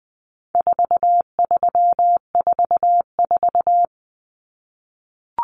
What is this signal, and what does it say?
Morse code, heard as a keyed tone, about 20 wpm, 700 Hz. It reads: 4344
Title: Common Cut Numbers - 4-Digit 20wpm